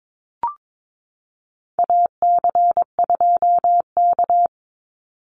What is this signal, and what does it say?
Morse code, heard as a keyed tone, about 22 wpm, 700 Hz. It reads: AC2K
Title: Call Signs - 22wpm